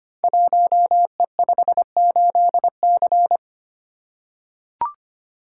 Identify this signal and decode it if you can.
Morse code, heard as a keyed tone, about 25 wpm, 700 Hz. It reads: 1E58C
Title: Code Groups - 5 Letter:Numbers - 25wpm